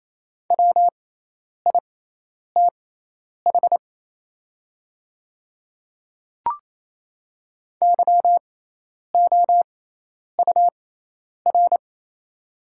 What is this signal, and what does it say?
Morse code, heard as a keyed tone, about 28 wpm, spaced as for 10 wpm, 690 Hz. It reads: WITH YOUR
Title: 6x Character Spacing - Top 100 Words - 28wpm